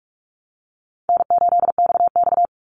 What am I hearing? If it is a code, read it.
Morse code, heard as a keyed tone, about 44 wpm, 700 Hz. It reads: N8XX
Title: Call Signs Repeated 2x - 45wpm